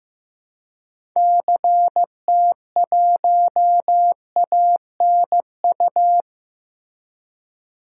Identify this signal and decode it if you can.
Morse code, heard as a keyed tone, about 16 wpm, 700 Hz. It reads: CT1ANU
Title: DXCC-100 Call Signs - 6 Characters - Repeated 3x - 15wpm